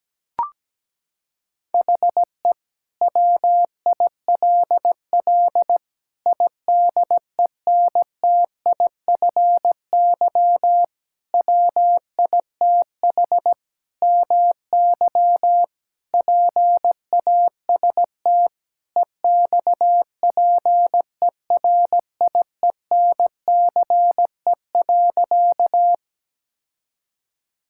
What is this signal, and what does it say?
Morse code, heard as a keyed tone, about 17 wpm, 700 Hz. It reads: HE WILL IDENTIFY WITH MY PAST EXPERIENCE.
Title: Sentences from Top 1000 Words - Review Part 6 - 17wpm